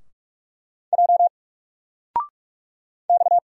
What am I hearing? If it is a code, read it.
Morse code, heard as a keyed tone, about 44 wpm, 700 Hz. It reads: J X
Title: Single Letter-Number Alt 45wpm